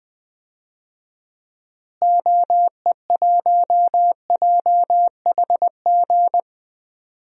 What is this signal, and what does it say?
Morse code, heard as a keyed tone, about 20 wpm, 700 Hz. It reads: OE1JHG
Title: DXCC-100 Call Signs - 6 Characters - Repeated 3x - 20wpm